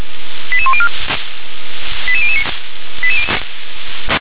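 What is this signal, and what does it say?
Five-tone selective call (ZVEI), heard as quick runs of five tones, tones of about 70 ms. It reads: 88104 88009
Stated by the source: ZVEI five-tone selective call 88040 received by SDR.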